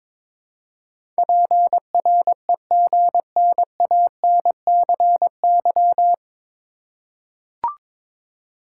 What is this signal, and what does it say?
Morse code, heard as a keyed tone, about 22 wpm, 700 Hz. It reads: PREGNANCY
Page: Morse Code Ninja practice 22wpm